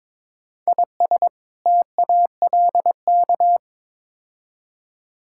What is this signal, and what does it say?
Morse code, heard as a keyed tone, about 22 wpm, 700 Hz. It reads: IS TALK
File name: practice / 22wpm / Build A Phrase - 5-Word Phrases - 22wpm